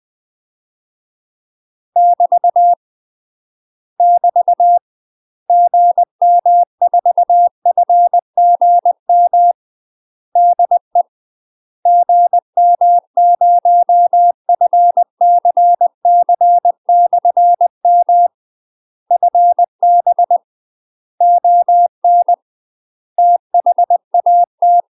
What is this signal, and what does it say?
Morse code, heard as a keyed tone, about 20 wpm, 700 Hz. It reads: <BT> <BT> GM4FGM DE GM0FCC/M FB ON THAT
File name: UK MASTER QSOs TRAINING / 20WPM / UK MASTER QSOs 0073